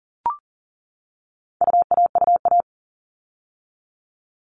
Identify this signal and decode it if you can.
Morse code, heard as a keyed tone, about 40 wpm, 700 Hz. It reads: UAUA
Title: All Cut Numbers - 4-Digit 40wpm